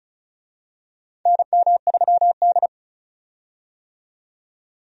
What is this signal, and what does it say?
Morse code, heard as a keyed tone, about 34 wpm, 700 Hz. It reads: NM3D
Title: USA Call Signs - 4 Characters - 35wpm